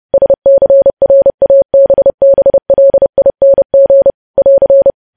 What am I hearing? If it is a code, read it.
Morse code, heard as a keyed tone, about 30 wpm, 550 Hz. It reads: SCRABBLING <AR>
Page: news headlines in morse code at 30 wpm